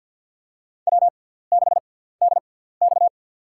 Morse code, heard as a keyed tone, about 51 wpm, spaced as for 17 wpm, 700 Hz. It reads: W6DX
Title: USA Call Signs - 4 Characters - 6x Character Spacing - 50wpm